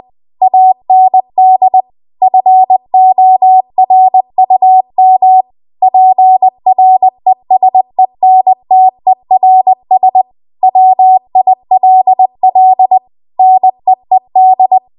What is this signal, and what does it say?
Morse code, heard as a keyed tone, about 20 wpm, 750 Hz. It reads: AND FORUM PRESENTERS WILL NEED